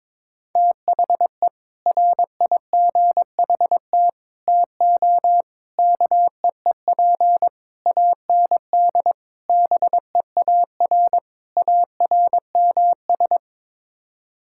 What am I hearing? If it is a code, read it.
Morse code, heard as a keyed tone, about 21 wpm, 700 Hz. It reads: THE RIGHT TO KEEP AND BEAR ARMS
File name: practice / 22wpm / Sets of 7 Words - 22wpm